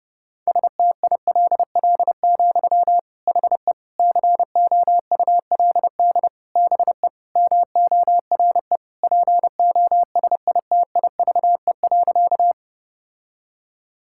Morse code, heard as a keyed tone, about 30 wpm, 700 Hz. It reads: STILL, HE COULD BE MORE POSITIVE.
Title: Sentences from Top 1000 Words - Review Part 5 - 30wpm